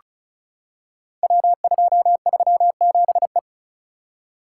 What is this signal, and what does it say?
Morse code, heard as a keyed tone, about 36 wpm, 700 Hz. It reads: W237E
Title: Code Groups - 5 Letter:Numbers - 35wpm